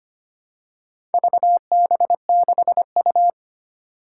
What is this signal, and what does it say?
Morse code, heard as a keyed tone, about 25 wpm, 700 Hz. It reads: VB6U
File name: practice / 25wpm / Four Letter-Number Combinations - 25wpm